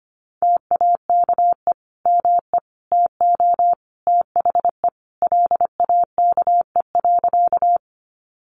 Morse code, heard as a keyed tone, about 25 wpm, 700 Hz. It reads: TAKE ME TO THE LAKE.
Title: Sentences from Top 1100 Words 25wpm